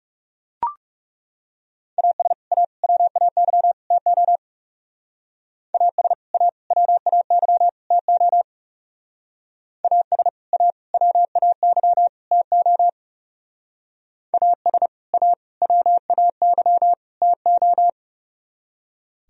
Morse code, heard as a keyed tone, about 35 wpm, 700 Hz. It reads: AS A WAY TO AS A WAY TO AS A WAY TO AS A WAY TO
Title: Speed-Racing - Sets of 4 Words 30wpm